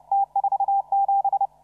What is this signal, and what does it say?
Morse code, heard as a keyed tone, about 33 wpm, 800 Hz. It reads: T47
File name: CW / RHOB CW 2017 HA6P 0001